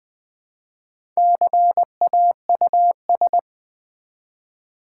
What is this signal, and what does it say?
Morse code, heard as a keyed tone, about 20 wpm, 700 Hz. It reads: CAUS
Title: Build A Word - 1 Letter At A Time - 3 to 5-Letter Words - 20wpm